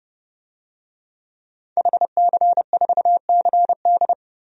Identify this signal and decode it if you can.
Morse code, heard as a keyed tone, about 30 wpm, 700 Hz. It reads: HC4CD
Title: DXCC-100 Call Signs - 5 Characters - 30wpm